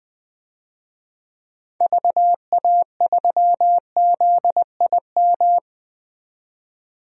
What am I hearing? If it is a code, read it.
Morse code, heard as a keyed tone, about 20 wpm, 700 Hz. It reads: VA3ZIM
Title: DXCC-100 Call Signs - 6 Characters - 20wpm